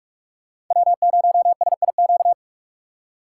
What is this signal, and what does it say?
Morse code, heard as a keyed tone, about 46 wpm, 700 Hz. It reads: W0SIQ